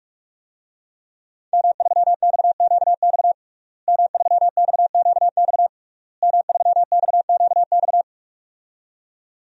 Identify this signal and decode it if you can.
Morse code, heard as a keyed tone, about 46 wpm, 700 Hz. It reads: M3XQX M3XQX M3XQX
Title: DXCC-100 Call Signs - 5 Characters - Repeated 3x - 45wpm